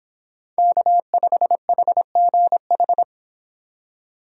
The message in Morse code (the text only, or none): K5HGH